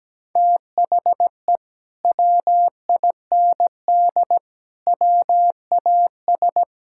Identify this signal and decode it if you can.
Morse code, heard as a keyed tone, about 17 wpm, 700 Hz. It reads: THE WIND WAS
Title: Sentences from Top 1600 Words 17wpm